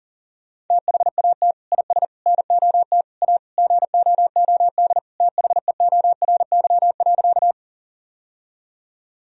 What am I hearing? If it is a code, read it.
Morse code, heard as a keyed tone, about 40 wpm, 700 Hz. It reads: THAT IS NOT A GOOD THEORY.
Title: Sentences from Top 1000 Words - Review Part 3 - 40wpm